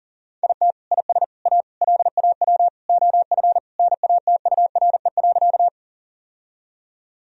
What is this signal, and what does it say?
Morse code, heard as a keyed tone, about 40 wpm, 700 Hz. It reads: IT IS A LAW OF NATURE.
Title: Sentences from Top 1000 Words - Review Part 3 - 40wpm